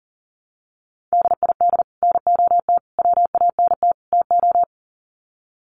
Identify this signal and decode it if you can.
Morse code, heard as a keyed tone, about 40 wpm, 700 Hz. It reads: DID NOT WANT TO
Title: Sets of 4 Words 40wpm